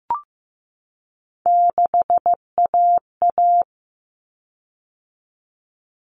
Morse code, heard as a keyed tone, about 15 wpm, 700 Hz. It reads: 6AA
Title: Common Cut Numbers - 3-Digit 15wpm